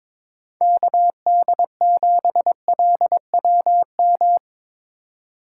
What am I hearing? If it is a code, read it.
Morse code, heard as a keyed tone, about 24 wpm, 700 Hz. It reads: KD7LWM
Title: Speed Racing - USA 2x3 Call Signs 22wpm